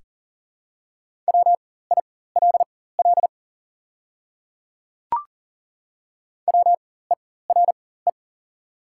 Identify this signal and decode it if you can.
Morse code, heard as a keyed tone, about 40 wpm, spaced as for 19 wpm, 690 Hz. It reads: WILL WERE
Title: 4x Character Spacing - Top 100 Words - 40wpm